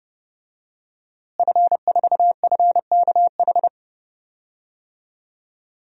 Morse code, heard as a keyed tone, about 29 wpm, 700 Hz. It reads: F4FKH
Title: DXCC-100 Call Signs - 5 Characters - Repeated 3x - 30wpm